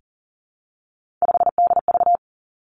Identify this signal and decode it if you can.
Morse code, heard as a keyed tone, about 38 wpm, 700 Hz. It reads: 5DV
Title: Three Challenging Letter-Number Combinations 40wpm